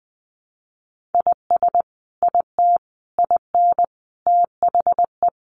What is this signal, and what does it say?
Morse code, heard as a keyed tone, about 20 wpm, 700 Hz. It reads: IS IT IN THE
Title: Sentences from Top 1200 Words 20wpm